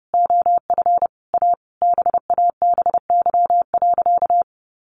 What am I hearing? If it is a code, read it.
Morse code, heard as a keyed tone, about 30 wpm, 700 Hz. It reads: OF A BABY.
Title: Sentences from Top 1400 Words 30wpm